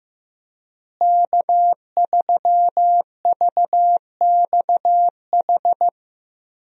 Morse code, heard as a keyed tone, about 15 wpm, 700 Hz. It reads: K3VXH
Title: USA Call Signs - 5 Characters - Repeated 2x - 15wpm